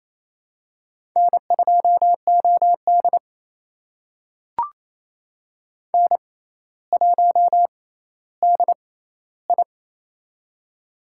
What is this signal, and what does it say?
Morse code, heard as a keyed tone, about 28 wpm, spaced as for 14 wpm, 700 Hz. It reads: N2OD N1DI
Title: USA Call Signs - 4 Characters - 6x Character Spacing - 28wpm